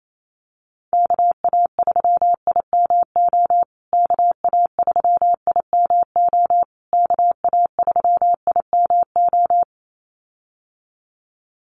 Morse code, heard as a keyed tone, about 28 wpm, 700 Hz. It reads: KA3IMO KA3IMO KA3IMO
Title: USA Call Signs Repeated 3x 28wpm